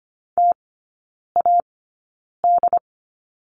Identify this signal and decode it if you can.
Morse code, heard as a keyed tone, about 25 wpm, spaced as for 9 wpm, 690 Hz. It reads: TAD
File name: Lesson 14e - Number 2 - Three Characters 25wpm